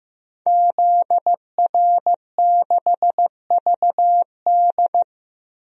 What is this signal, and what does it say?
Morse code, heard as a keyed tone, about 15 wpm, 700 Hz. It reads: ZR6VD
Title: DXCC-100 Call Signs - 5 Characters - Repeated 2x - 15wpm